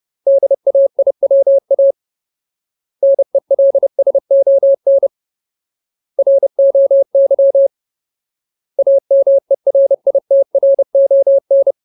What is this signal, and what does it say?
Morse code, heard as a keyed tone, about 30 wpm, 550 Hz. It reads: DAIWA NELSON ROY AMERITRON
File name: Ultimate Word List / 30WPM Ultimate